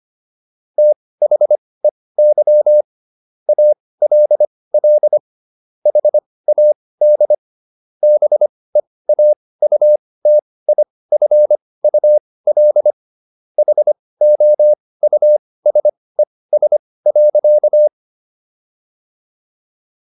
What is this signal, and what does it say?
Morse code, heard as a keyed tone, about 25 wpm, spaced as for 18 wpm, 600 Hz. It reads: THEY ALL HAD BEAUTIFUL HOUSES.